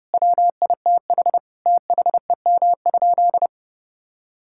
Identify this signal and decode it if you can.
Morse code, heard as a keyed tone, about 30 wpm, 700 Hz. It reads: WITH THEM?
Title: Sentences from Top 1000 Words - Review Part 6 - 30wpm